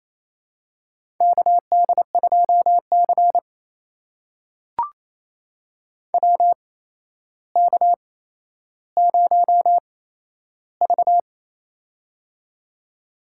Morse code, heard as a keyed tone, about 28 wpm, spaced as for 12 wpm, 700 Hz. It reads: KD2C WK0V
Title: USA Call Signs - 4 Characters - 8x Character Spacing - 28wpm